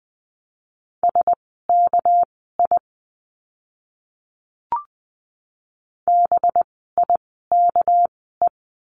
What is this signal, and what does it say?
Morse code, heard as a keyed tone, about 20 wpm, spaced as for 14 wpm, 700 Hz. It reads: SKI BIKE
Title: 199 Lesson 32f - Letter k - Words 20wpm